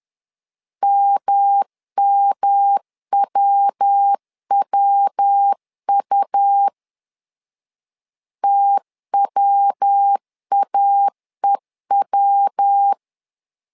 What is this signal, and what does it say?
Morse code, heard as a keyed tone, about 10 wpm, 800 Hz. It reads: MMWWU TWAEW
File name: LEZIONI / 3 MWRO / 3 MW 10wpm 1959